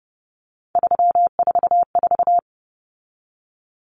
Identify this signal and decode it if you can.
Morse code, heard as a keyed tone, about 30 wpm, 700 Hz. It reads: 344
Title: Numbers - 3-Digit 30wpm